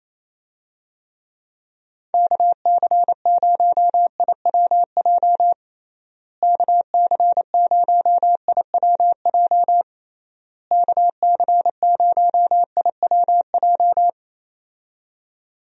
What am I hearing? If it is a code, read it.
Morse code, heard as a keyed tone, about 28 wpm, 700 Hz. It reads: KC0IWJ KC0IWJ KC0IWJ
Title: USA Call Signs - 6 Characters - Repeated 3x - 28wpm